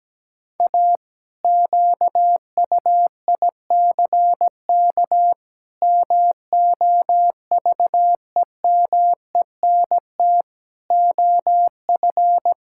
Morse code, heard as a keyed tone, about 17 wpm, 700 Hz. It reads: A QUICK MOVEMENT OF
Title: Pangram Challenge v2 - 17wpm